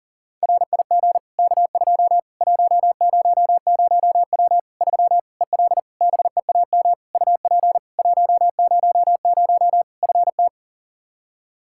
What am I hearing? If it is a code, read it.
Morse code, heard as a keyed tone, about 40 wpm, 700 Hz. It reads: RIG K2 100W 3 EL BEAM UP 100 FT
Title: K1USN SST QSO Practice 40wpm